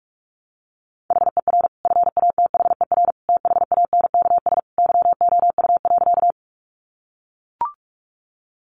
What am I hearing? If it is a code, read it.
Morse code, heard as a keyed tone, about 44 wpm, 700 Hz. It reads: HER FATHER THANKS YOU.
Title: Sentences from Top 600 Words 45wpm